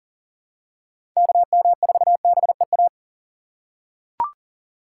Morse code, heard as a keyed tone, about 39 wpm, 700 Hz. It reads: KM4BEA
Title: USA Call Signs - 6 Characters - 2x Character Spacing - 40wpm